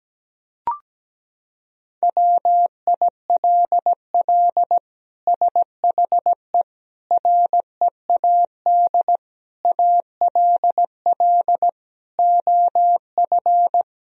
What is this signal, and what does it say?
Morse code, heard as a keyed tone, about 17 wpm, 700 Hz. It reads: WILL SHE READ ALL OF
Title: Sentences from Top 1000 Words - Review Part 4 - 17wpm